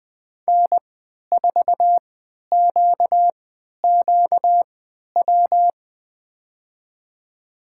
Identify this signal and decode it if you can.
Morse code, heard as a keyed tone, about 21 wpm, spaced as for 11 wpm, 700 Hz. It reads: N4QQW
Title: USA Call Signs - 5 Characters - 3x Character Spacing - 20wpm